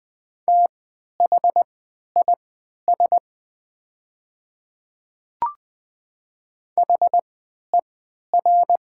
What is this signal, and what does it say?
Morse code, heard as a keyed tone, about 20 wpm, spaced as for 11 wpm, 700 Hz. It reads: THIS HER
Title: 3x Character Spacing - Top 100 Words - 20wpm